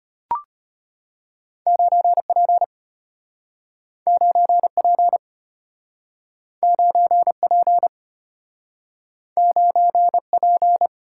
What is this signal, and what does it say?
Morse code, heard as a keyed tone, about 35 wpm, 700 Hz. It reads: 9P 9P 9P 9P